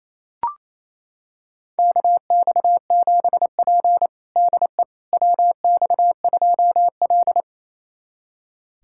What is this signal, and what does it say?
Morse code, heard as a keyed tone, about 27 wpm, 690 Hz. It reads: KX7P DE WX2L
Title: USA Callsign DE Callsign - 4 Characters - 28wpm